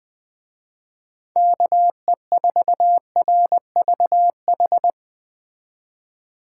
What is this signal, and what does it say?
Morse code, heard as a keyed tone, about 20 wpm, 700 Hz. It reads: KE4RVH